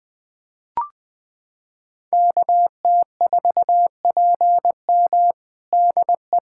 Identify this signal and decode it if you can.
Morse code, heard as a keyed tone, about 20 wpm, 700 Hz. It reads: KT4PM DE
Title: USA Callsign DE Callsign - 5 Characters - 20wpm